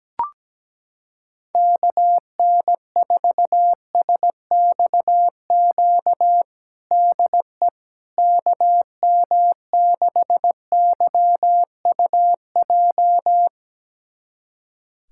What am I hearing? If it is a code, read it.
Morse code, heard as a keyed tone, about 17 wpm, 700 Hz. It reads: KN4SXQ DE KM6YUJ
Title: USA Callsign DE Callsign 17wpm